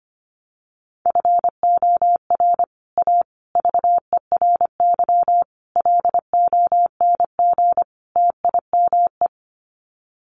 Morse code, heard as a keyed tone, about 25 wpm, 700 Hz. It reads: FOR A VERY LONG TIME
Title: Sets of 5 Words 25wpm